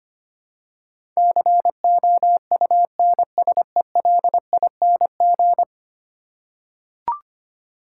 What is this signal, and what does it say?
Morse code, heard as a keyed tone, about 25 wpm, 700 Hz. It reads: COUNSELING
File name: 10 Letter Words - First 3 Letters Given - 25wpm